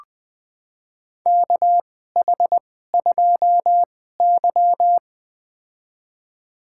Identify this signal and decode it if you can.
Morse code, heard as a keyed tone, about 20 wpm, spaced as for 15 wpm, 700 Hz. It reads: KH2Y
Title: USA Call Signs - 4 Characters - 2x Character Spacing - 20wpm